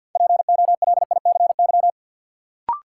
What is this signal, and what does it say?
Morse code, heard as a keyed tone, about 47 wpm, 700 Hz. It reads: POLICY
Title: Sentences from Top 1000 Words - Review Part 5 - New Word Drill - 50wpm